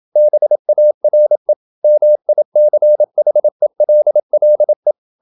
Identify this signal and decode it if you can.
Morse code, heard as a keyed tone, about 27 wpm, 600 Hz. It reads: BARE MICHELLE
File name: Ferie QRQ 27wpm